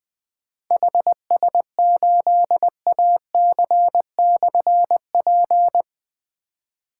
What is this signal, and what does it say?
Morse code, heard as a keyed tone, about 20 wpm, 700 Hz. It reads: HS8AC/P
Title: Difficult Call Signs 20wpm